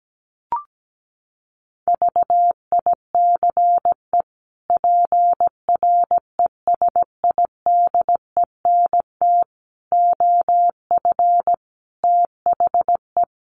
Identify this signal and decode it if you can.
Morse code, heard as a keyed tone, about 17 wpm, 700 Hz. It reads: VICE PRESIDENT OF THE
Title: Sets of 5 Words Encore2 17wpm